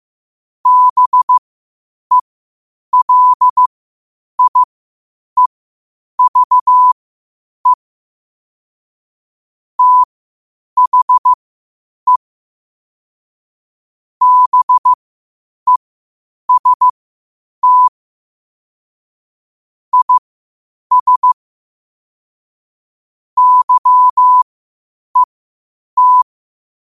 Morse code, heard as a keyed tone, about 15 wpm, spaced as for 8 wpm, 1000 Hz. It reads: BELIEVE THE BEST IS YET
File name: qotdfiles / 8wpm